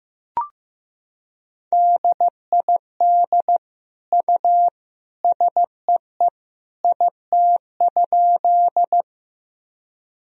Morse code, heard as a keyed tone, about 15 wpm, 700 Hz. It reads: DID U SEE IT?
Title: Sentences from Top 100 Words and Abbreviations 15wpm